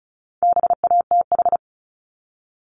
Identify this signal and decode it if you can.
Morse code, heard as a keyed tone, about 34 wpm, 700 Hz. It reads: BATH
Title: Top 1501-1750 Words 35wpm